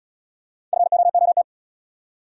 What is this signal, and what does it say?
Morse code, heard as a keyed tone, about 74 wpm, 700 Hz. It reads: FOOT